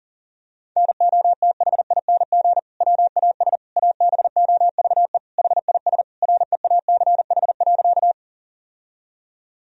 Morse code, heard as a keyed tone, about 40 wpm, 700 Hz. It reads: NOTHING WAS ABOVE HIS REACH.
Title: Sentences from Top 1000 Words - Review Part 2 - 40wpm